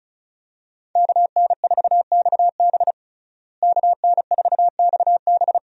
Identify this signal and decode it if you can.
Morse code, heard as a keyed tone, about 34 wpm, 700 Hz. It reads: KN4XB KN4XB
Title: USA Call Signs - 5 Characters - Repeated 3x - 35wpm